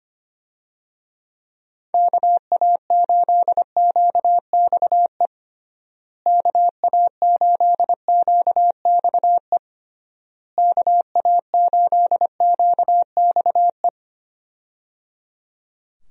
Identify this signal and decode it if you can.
Morse code, heard as a keyed tone, about 25 wpm, 700 Hz. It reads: KA8QXE KA8QXE KA8QXE
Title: USA Call Signs - 6 Characters - Repeated 3x - 25wpm